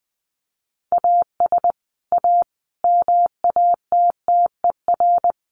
Morse code, heard as a keyed tone, about 20 wpm, 700 Hz. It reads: AS A MATTER
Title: Sets of 5 Words 20wpm